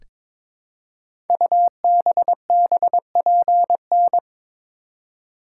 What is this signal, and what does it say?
Morse code, heard as a keyed tone, about 22 wpm, 700 Hz. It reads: UBBPN
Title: Code Groups - 5 Letter:Numbers - 22wpm